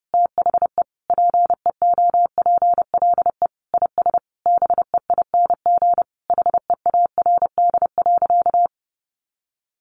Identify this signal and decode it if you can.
Morse code, heard as a keyed tone, about 28 wpm, 700 Hz. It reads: THE PEOPLE IS BEING HEARD.
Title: Sentences from Top 1500 Words 30wpm